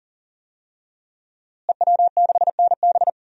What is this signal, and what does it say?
Morse code, heard as a keyed tone, about 40 wpm, 700 Hz. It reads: EW6NB
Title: DXCC-100 Call Signs - 5 Characters - Repeated 3x - 40wpm